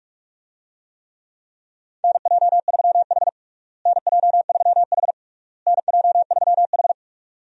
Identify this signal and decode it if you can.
Morse code, heard as a keyed tone, about 46 wpm, 700 Hz. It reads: NJ3H NJ3H NJ3H
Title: USA Call Signs - 4 Characters - Repeated 3x - 45wpm